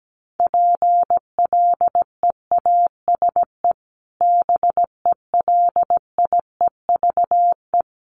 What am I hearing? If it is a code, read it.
Morse code, heard as a keyed tone, about 17 wpm, 700 Hz. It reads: PLEASE BELIEVE
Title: Sentences from Top 1500 Words 17wpm